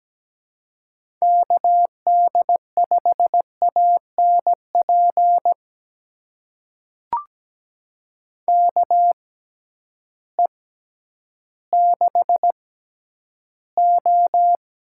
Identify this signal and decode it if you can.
Morse code, heard as a keyed tone, about 17 wpm, spaced as for 10 wpm, 700 Hz. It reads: KD5ANP KE6O
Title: USA Call Signs - 6 Characters - 6x Character Spacing - 17wpm